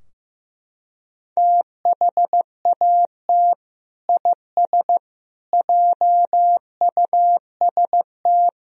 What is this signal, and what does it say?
Morse code, heard as a keyed tone, about 15 wpm, 700 Hz. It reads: THAT IS JUST
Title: Sentences from Top 1000 Words - Review Part 4 - 15wpm